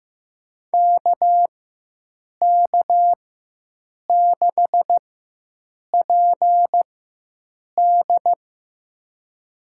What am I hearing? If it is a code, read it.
Morse code, heard as a keyed tone, about 15 wpm, spaced as for 7 wpm, 700 Hz. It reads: KK6PD
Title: USA Call Signs - 4x Character Spacing - 15wpm